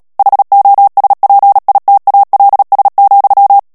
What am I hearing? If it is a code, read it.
Morse code, heard as a keyed tone, about 37 wpm, 800 Hz. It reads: HOSPITALS,
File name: morse mp3 / ANGLAIS texte clair 2 vitesse 35 mots minute